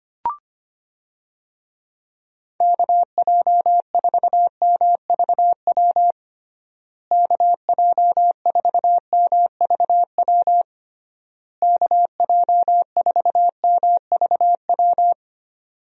Morse code, heard as a keyed tone, about 25 wpm, 700 Hz. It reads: KJ4MVW KJ4MVW KJ4MVW
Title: USA Call Signs - 6 Characters - Repeated 3x - 25wpm